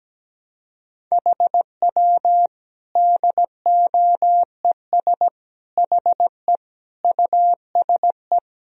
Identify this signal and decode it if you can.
Morse code, heard as a keyed tone, about 17 wpm, 700 Hz. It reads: HW DOES HE USE
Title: Sentences from Top 200 Words and Abbreviations 17wpm